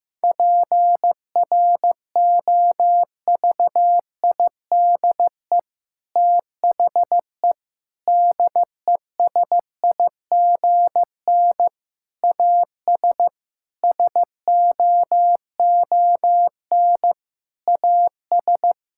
Morse code, heard as a keyed tone, about 15 wpm, 700 Hz. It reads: PROVIDE THE DESIGN AS SOON AS
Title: Sentences from Top 1000 Words - Review Part 5 - 15wpm